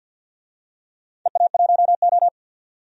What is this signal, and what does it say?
Morse code, heard as a keyed tone, about 52 wpm, 700 Hz. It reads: EA1O
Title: DXCC-100 Call Signs - 4 Characters - 50wpm